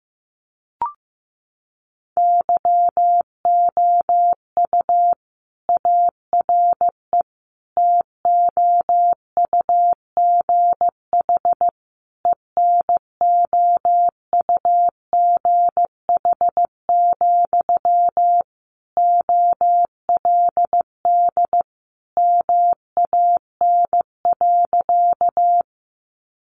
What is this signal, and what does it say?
Morse code, heard as a keyed tone, about 15 wpm, 700 Hz. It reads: YOU ARE TOUGH ENOUGH, OLD MAN.
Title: Sentences from Top 1200 Words 15wpm